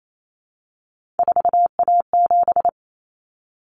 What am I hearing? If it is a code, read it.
Morse code, heard as a keyed tone, about 28 wpm, 700 Hz. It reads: 4A7
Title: Common Cut Numbers - 3-Digit 28wpm